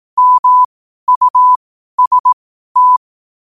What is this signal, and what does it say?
Morse code, heard as a keyed tone, about 18 wpm, spaced as for 12 wpm, 1000 Hz. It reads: MUST
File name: qotdfiles / 12wpm